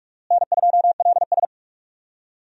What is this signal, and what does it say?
Morse code, heard as a keyed tone, about 42 wpm, 700 Hz. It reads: N2LS
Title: USA Call Signs - 4 Characters - Repeated 2x - 45wpm